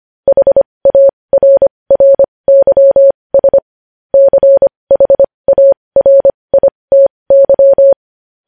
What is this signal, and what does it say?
Morse code, heard as a keyed tone, about 25 wpm, spaced as for 20 wpm, 550 Hz. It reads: HARRYS CHARITY
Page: news headlines in morse code at 20 wpm